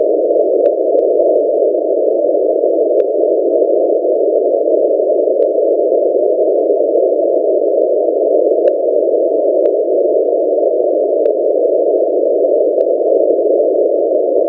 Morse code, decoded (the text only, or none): none